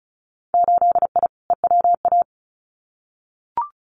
Morse code, heard as a keyed tone, about 34 wpm, 700 Hz. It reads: 8I EWA
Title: Winter Field Day Exchanges Encore 35wpm